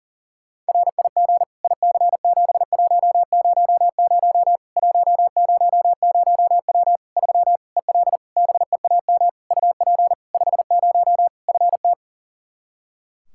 Morse code, heard as a keyed tone, about 40 wpm, 700 Hz. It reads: RIG IC7100 100W 3 EL BEAM UP 50 FT
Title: Practice QSO - Exchange 2 - Info - 40wpm